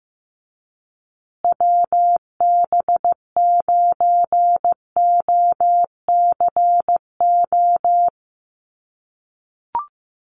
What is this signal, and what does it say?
Morse code, heard as a keyed tone, about 15 wpm, 700 Hz. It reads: WB9OCO
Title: USA Call Signs Repeated 3x 15wpm